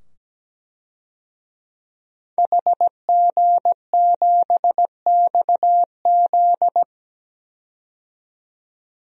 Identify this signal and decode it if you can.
Morse code, heard as a keyed tone, about 17 wpm, 700 Hz. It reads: HG7XZ
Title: DXCC-100 Call Signs - 5 Characters - 17wpm